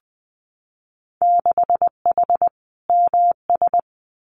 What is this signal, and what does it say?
Morse code, heard as a keyed tone, about 20 wpm, 700 Hz. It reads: 6H MS
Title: Winter Field Day Exchanges Encore 20wpm